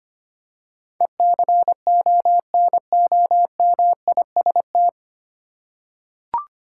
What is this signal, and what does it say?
Morse code, heard as a keyed tone, about 25 wpm, 690 Hz. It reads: ECONOMIST
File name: practice / 25wpm / 9 Letter Words - First 3 Letters Given - 25wpm